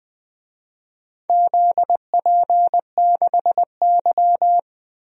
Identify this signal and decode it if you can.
Morse code, heard as a keyed tone, about 20 wpm, 700 Hz. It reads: ZP6Y
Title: DXCC-100 Call Signs - 4 Characters - Repeated 2x - 20wpm